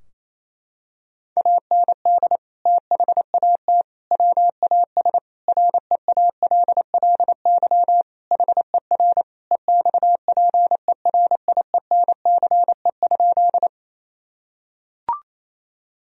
Morse code, heard as a keyed tone, about 28 wpm, 700 Hz. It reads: AND THAT WAS REALLY HER EXPERIENCE?
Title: Sentences from Top 1000 Words - Review Part 3 - 28wpm